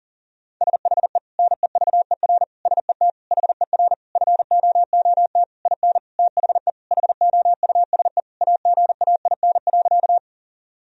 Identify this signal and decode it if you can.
Morse code, heard as a keyed tone, about 40 wpm, 700 Hz. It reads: SHE NEVER SET HER FOOT IN THE HOUSE AGAIN.
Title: Sentences from Top 1000 Words - Review Part 3 - 40wpm